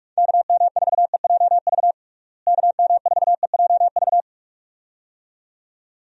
Morse code, heard as a keyed tone, about 46 wpm, 700 Hz. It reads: KM4EJV KM4EJV
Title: USA Call Signs - 6 Characters - Repeated 2x - 45wpm